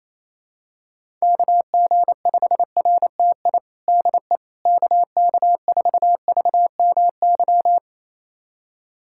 Morse code, heard as a keyed tone, about 28 wpm, 700 Hz. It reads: KG5RTI DE KK4VMY
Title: USA Callsign DE Callsign 28wpm